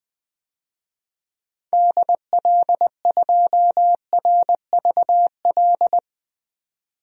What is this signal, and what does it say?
Morse code, heard as a keyed tone, about 20 wpm, 700 Hz. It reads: DL2RVL
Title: DXCC-100 Call Signs - 6 Characters - 20wpm